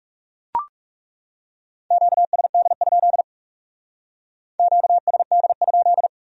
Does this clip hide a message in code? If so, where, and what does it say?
Yes: Morse code from 1.9 to 6.1 s, heard as a keyed tone, about 39 wpm, 700 Hz. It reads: QSD? QSD?